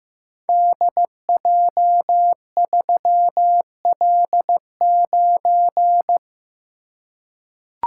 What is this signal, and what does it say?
Morse code, heard as a keyed tone, about 15 wpm, 700 Hz. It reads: DJ3L9
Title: Code Groups - 5 Letter:Numbers - 15wpm